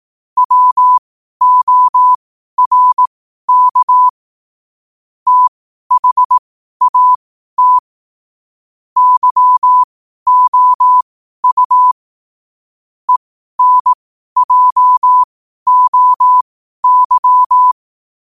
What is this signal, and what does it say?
Morse code, heard as a keyed tone, about 18 wpm, spaced as for 12 wpm, 1000 Hz. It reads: WORK THAT YOU ENJOY
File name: qotdfiles / 12wpm